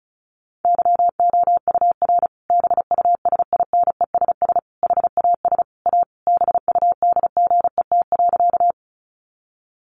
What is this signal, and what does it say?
Morse code, heard as a keyed tone, about 34 wpm, 700 Hz. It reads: YOUR BUSINESS HAS A BUDGET.
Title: Sentences from Top 1200 Words 35wpm